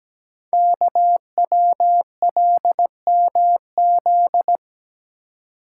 Morse code, heard as a keyed tone, about 17 wpm, 700 Hz. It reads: KWLMZ
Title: Code Groups - 5 Letter:Numbers - 17wpm